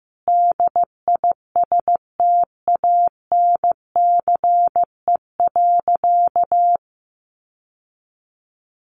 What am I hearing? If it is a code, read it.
Morse code, heard as a keyed tone, about 15 wpm, 700 Hz. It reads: DISTANCE.
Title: Sentences from Top 1200 Words 15wpm